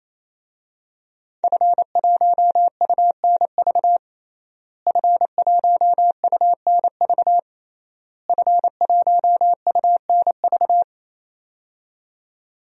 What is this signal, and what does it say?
Morse code, heard as a keyed tone, about 27 wpm, 700 Hz. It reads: F1UNV F1UNV F1UNV
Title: DXCC-100 Call Signs - 5 Characters - Repeated 3x - 28wpm